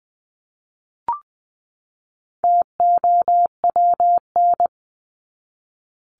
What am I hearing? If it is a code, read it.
Morse code, heard as a keyed tone, about 20 wpm, 700 Hz. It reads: TOWN
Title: Top 500 Words 20wpm